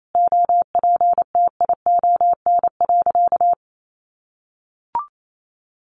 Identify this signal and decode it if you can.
Morse code, heard as a keyed tone, about 28 wpm, 700 Hz. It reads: OPTION.
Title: Sentences from Top 1300 Words 28wpm